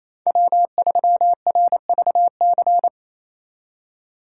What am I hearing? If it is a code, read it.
Morse code, heard as a keyed tone, about 34 wpm, 700 Hz. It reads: W3RVC
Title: Speed Racing - USA 1x3 Call Signs 28wpm